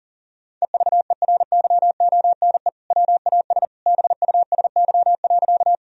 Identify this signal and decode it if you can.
Morse code, heard as a keyed tone, about 40 wpm, 700 Hz. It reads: EVERYONE WAS BUSY.
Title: Sentences from Top 1000 Words - Review Part 3 - 40wpm